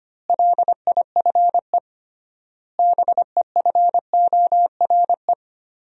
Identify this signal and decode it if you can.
Morse code, heard as a keyed tone, about 25 wpm, 700 Hz. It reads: LIFE BEFORE
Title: 3x Word-Spacing - Sentences from Top 400 Words - 25wpm